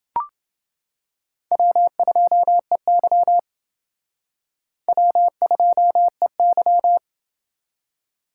Morse code, heard as a keyed tone, about 28 wpm, 700 Hz. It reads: W2EY W2EY